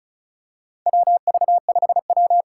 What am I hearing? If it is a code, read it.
Morse code, heard as a keyed tone, about 36 wpm, 690 Hz. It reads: WV5W
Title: USA Call Signs - 3x Character Spacing - 35wpm